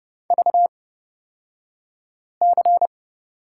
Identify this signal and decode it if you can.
Morse code, heard as a keyed tone, about 31 wpm, spaced as for 5 wpm, 700 Hz. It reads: VC
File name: Lesson 36d - BT - Two Characters 30wpm